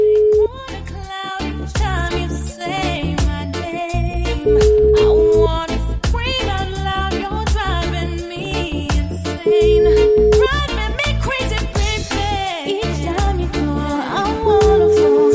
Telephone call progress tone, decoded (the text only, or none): ringback tone